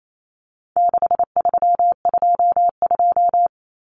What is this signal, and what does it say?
Morse code, heard as a keyed tone, about 28 wpm, 700 Hz. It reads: 6322
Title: Numbers - 4-Digit 28wpm